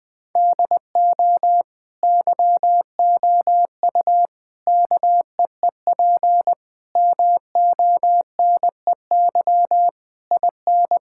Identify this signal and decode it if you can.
Morse code, heard as a keyed tone, about 20 wpm, 700 Hz. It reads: DO YOU KEEP MONEY IN
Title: Sentences from Top 1000 Words - Review Part 6 - 20wpm